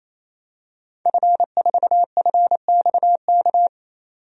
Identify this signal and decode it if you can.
Morse code, heard as a keyed tone, about 27 wpm, 700 Hz. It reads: F4FXK
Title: DXCC-100 Call Signs - 5 Characters - 28wpm